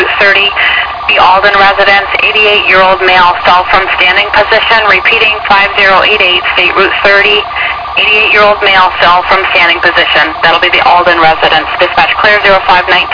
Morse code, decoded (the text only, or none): none